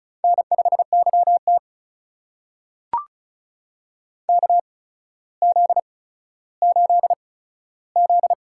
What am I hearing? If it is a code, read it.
Morse code, heard as a keyed tone, about 36 wpm, spaced as for 15 wpm, 700 Hz. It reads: N5YT KZ8Z
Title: USA Call Signs - 4 Characters - 8x Character Spacing - 35wpm